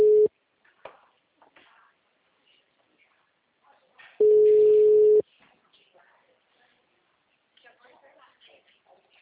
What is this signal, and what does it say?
Ringback tone (Europe, 425 Hz), with speech.